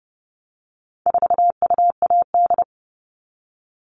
Morse code, heard as a keyed tone, about 30 wpm, 700 Hz. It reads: 4UAD
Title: All Cut Numbers - 4-Digit 30wpm